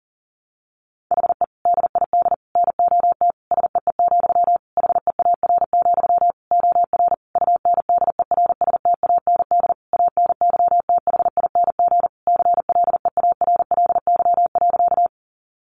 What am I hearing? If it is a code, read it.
Morse code, heard as a keyed tone, about 37 wpm, 700 Hz. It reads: HE DID NOT SEE, HEAR, OR UNDERSTAND ANYTHING CLEARLY.
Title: Sentences from Top 1300 Words 40wpm